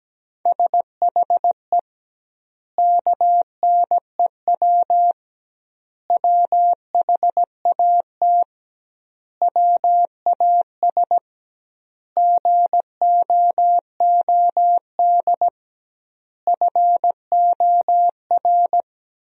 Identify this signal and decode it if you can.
Morse code, heard as a keyed tone, about 17 wpm, 700 Hz. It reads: SHE KNEW WHAT WAS GOOD FOR
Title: 2x Word-Spacing - Sentences from Top 400 Words - 17wpm